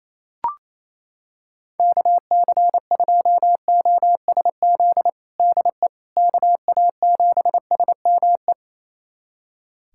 Morse code, heard as a keyed tone, about 28 wpm, 700 Hz. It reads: KC2OSZ DE KA7SME
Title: USA Callsign DE Callsign - 6 Characters - 28wpm